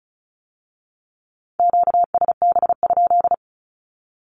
Morse code, heard as a keyed tone, about 36 wpm, 700 Hz. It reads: QSB?
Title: Q-Codes with Spoken Meaning - 35wpm